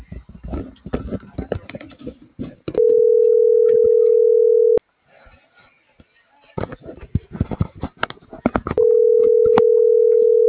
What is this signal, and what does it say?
Ringback tone (North America), 440 + 480 Hz.
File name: CENTRAL GENERAL / out arturo.vega